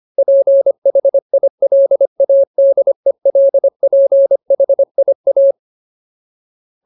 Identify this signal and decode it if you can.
Morse code, heard as a keyed tone, about 25 wpm, 550 Hz. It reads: PHILADELPHIA
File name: Ultimate Word List / 25WPM Ultimate